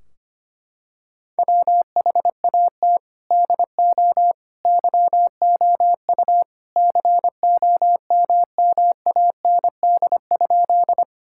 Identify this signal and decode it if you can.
Morse code, heard as a keyed tone, about 25 wpm, 700 Hz. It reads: WHAT DO YOU COMMAND?
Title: Sentences from Top 1600 Words 25wpm